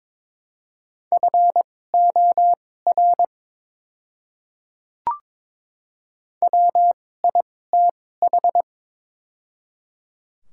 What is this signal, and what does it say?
Morse code, heard as a keyed tone, about 22 wpm, spaced as for 16 wpm, 700 Hz. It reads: FOR WITH
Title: 2x Character Spacing - Top 100 Words - 22wpm